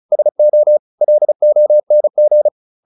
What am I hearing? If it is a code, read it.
Morse code, heard as a keyed tone, about 35 wpm, 600 Hz. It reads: SO LONG
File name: the old man and the sea 2 35wpm